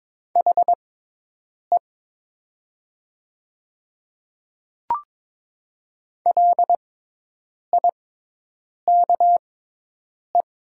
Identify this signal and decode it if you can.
Morse code, heard as a keyed tone, about 22 wpm, spaced as for 8 wpm, 700 Hz. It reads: HE LIKE
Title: 6x Character Spacing - Top 100 Words - 22wpm